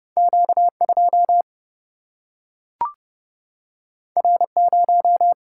Morse code, heard as a keyed tone, about 30 wpm, 700 Hz. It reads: Q2 R0
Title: Single-Letter Single-Number 30wpm